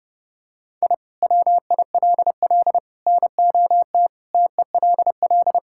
Morse code, heard as a keyed tone, about 32 wpm, 700 Hz. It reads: I WILL NOT TELL
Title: Sentences from Top 1000 Words - Review Part 5 - 30wpm